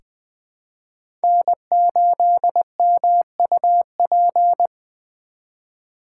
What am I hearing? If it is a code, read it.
Morse code, heard as a keyed tone, about 20 wpm, 700 Hz. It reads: N8MUP
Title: USA Call Signs - 5 Characters - 2x Character Spacing - 20wpm